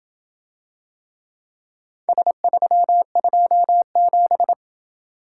Morse code, heard as a keyed tone, about 27 wpm, 700 Hz. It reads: S327